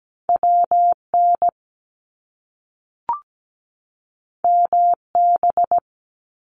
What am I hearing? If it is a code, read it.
Morse code, heard as a keyed tone, about 17 wpm, 700 Hz. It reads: WN MB
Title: Top Two Letter Combinations in Words 17wpm